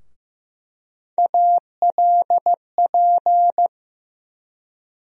Morse code, heard as a keyed tone, about 15 wpm, 700 Hz. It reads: ALP
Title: USA Call Sign Postfixes - 3 Letters 15wpm